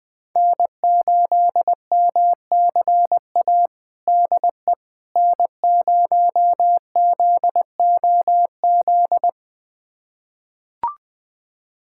Morse code, heard as a keyed tone, about 20 wpm, 700 Hz. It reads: N8MCA DE N0ZOZ
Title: USA Callsign DE Callsign - 5 Characters - 20wpm